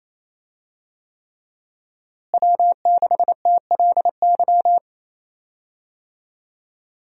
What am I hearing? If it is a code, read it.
Morse code, heard as a keyed tone, about 28 wpm, 700 Hz. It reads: W6TLY
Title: USA Call Signs - 5 Characters - Repeated 3x - 28wpm